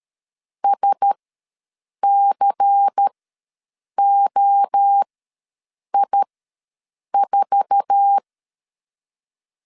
Morse code, heard as a keyed tone, about 13 wpm, spaced as for 7 wpm, 800 Hz. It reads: SCOI4
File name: LEZIONI / 10 SIGNS / 10A SLASH ws 12wpm 3443